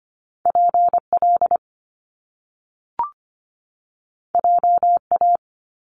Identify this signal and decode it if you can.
Morse code, heard as a keyed tone, about 25 wpm, 700 Hz. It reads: PL JA
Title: Top Two Letter Combinations in Words 25wpm